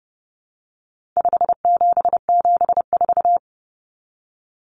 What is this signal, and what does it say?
Morse code, heard as a keyed tone, about 30 wpm, 700 Hz. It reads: 5774
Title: Common Cut Numbers - 4-Digit 30wpm